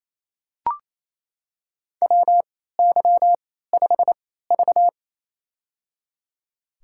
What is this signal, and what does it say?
Morse code, heard as a keyed tone, about 28 wpm, spaced as for 16 wpm, 700 Hz. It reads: WY5V
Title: USA Call Signs - 4 Characters - 3x Character Spacing - 28wpm